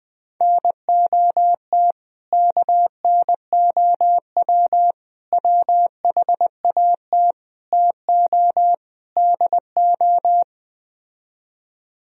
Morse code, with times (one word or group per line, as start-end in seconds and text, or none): 0.4-1.9 NOT
2.3-4.9 KNOW
5.3-7.3 WHAT
7.7-8.7 TO
9.2-10.4 DO